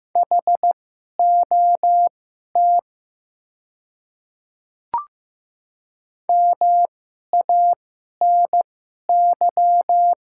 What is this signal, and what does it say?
Morse code, heard as a keyed tone, about 15 wpm, spaced as for 11 wpm, 700 Hz. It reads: HOT MANY
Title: 2x Character Spacing - Top 100 Words - 15wpm